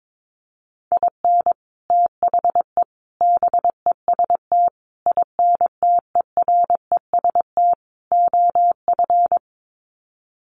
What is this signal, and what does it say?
Morse code, heard as a keyed tone, about 22 wpm, 700 Hz. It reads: IN THE BEST INTEREST OF
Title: Sets of 5 Words Encore 22wpm